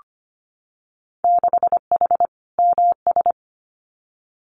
Morse code, heard as a keyed tone, about 25 wpm, 700 Hz. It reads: 6H MS
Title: Winter Field Day Exchanges Encore 25wpm